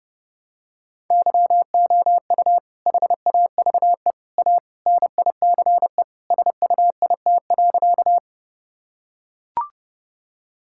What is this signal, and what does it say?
Morse code, heard as a keyed tone, about 30 wpm, 700 Hz. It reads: YOU HAVE A NICE SUIT.
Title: Sentences from Top 1600 Words 30wpm